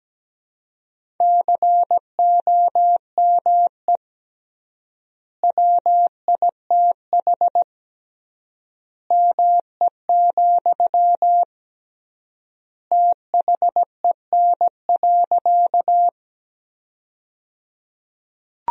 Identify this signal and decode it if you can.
Morse code, heard as a keyed tone, about 17 wpm, 700 Hz. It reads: COME WITH ME, THEN.
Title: 3x Word-Spacing - Sentences from Top 200 Words - 17wpm